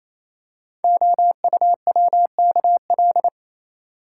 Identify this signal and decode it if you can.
Morse code, heard as a keyed tone, about 28 wpm, 700 Hz. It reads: OUWKL
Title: Code Groups - 5 Letters - 28wpm